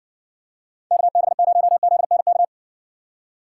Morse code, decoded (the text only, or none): KB0ZNK